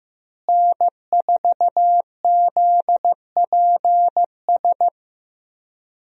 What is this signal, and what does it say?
Morse code, heard as a keyed tone, about 15 wpm, 700 Hz. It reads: N4ZPS
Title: USA Call Signs - 5 Characters - Repeated 3x - 15wpm